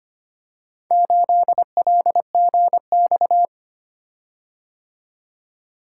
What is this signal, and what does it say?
Morse code, heard as a keyed tone, about 25 wpm, 700 Hz. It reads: 8LGX
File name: practice / 25wpm / Four Letter-Number Combinations - 25wpm